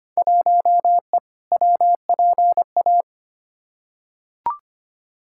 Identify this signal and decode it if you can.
Morse code, heard as a keyed tone, about 25 wpm, 700 Hz. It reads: 1E WPA
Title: Field Day Exchanges - 25wpm